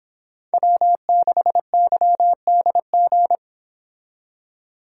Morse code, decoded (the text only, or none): W6YDG